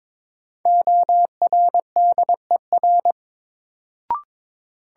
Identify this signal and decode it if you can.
Morse code, heard as a keyed tone, about 22 wpm, 700 Hz. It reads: ORDER
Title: Sets of 2 Words - New Word Drill - 22wpm